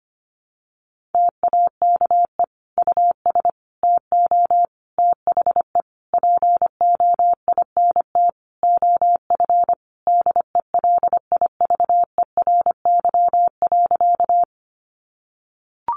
Morse code, heard as a keyed tone, about 25 wpm, 700 Hz. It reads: TAKE US TO THE POINT OF DELIVERY.
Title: Sentences from Top 1500 Words 25wpm